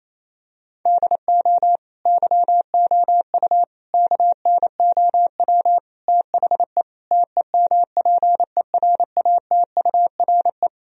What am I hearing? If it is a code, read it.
Morse code, heard as a keyed tone, about 28 wpm, 700 Hz. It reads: DO YOU KNOW THE TEMPERATURE
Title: Sentences from Top 1000 Words - Review Part 4 - 28wpm